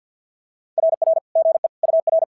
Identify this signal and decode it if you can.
Morse code, heard as a keyed tone, about 50 wpm, 650 Hz. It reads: RR GE UR
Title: SOTA QSO Practice 50wpm